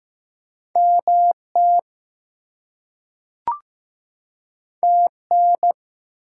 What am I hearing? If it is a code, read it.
Morse code, heard as a keyed tone, about 15 wpm, 700 Hz. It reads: MT TN
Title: US States and Canadian Provinces 15wpm